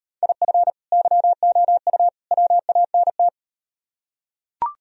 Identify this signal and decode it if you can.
Morse code, heard as a keyed tone, about 38 wpm, 700 Hz. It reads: IF YOU WANT